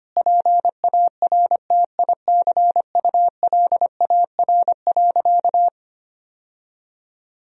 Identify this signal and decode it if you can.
Morse code, heard as a keyed tone, about 25 wpm, 700 Hz. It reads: PARTICULAR.
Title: Sentences from Top 1000 Words - Review Part 4 - 25wpm